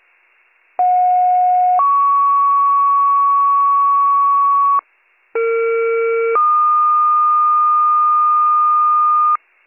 Two-tone page, heard as a one-second tone, then a three-second tone, about 730 then 1120 Hz, about 460 then 1220 Hz. It is heard from 0.8 to 9.4 s.